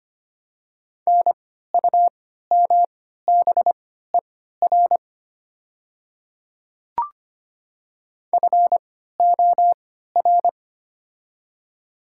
Morse code, heard as a keyed tone, about 25 wpm, spaced as for 14 wpm, 700 Hz. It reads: NUMBER FOR
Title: 3x Character Spacing - Top 100 Words - 25wpm